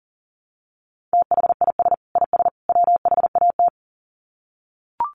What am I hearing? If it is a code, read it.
Morse code, heard as a keyed tone, about 40 wpm, 700 Hz. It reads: THIS IS WHAT
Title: Sets of 3 Words 40wpm